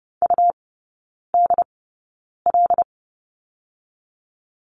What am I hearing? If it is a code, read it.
Morse code, heard as a keyed tone, about 30 wpm, 700 Hz. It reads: U D L
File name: Lesson 19e - Letter w - Three Characters 30wpm